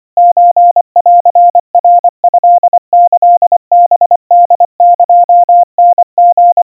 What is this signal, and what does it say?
Morse code (Japanese Wabun code), heard as a keyed tone, about 25 wpm, 700 Hz. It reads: ソンナトキハホエタリ
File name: 90 cpm 700 Hz Crispy